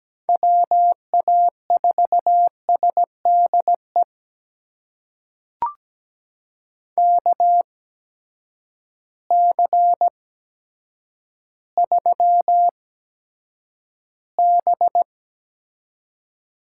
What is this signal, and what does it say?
Morse code, heard as a keyed tone, about 17 wpm, spaced as for 8 wpm, 700 Hz. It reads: WA4SDE KC3B
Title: USA Call Signs - 6 Characters - 8x Character Spacing - 17wpm